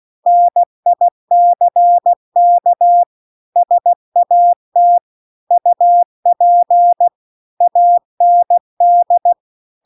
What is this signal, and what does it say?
Morse code, heard as a keyed tone, about 16 wpm, 700 Hz. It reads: NICK SAT UP AND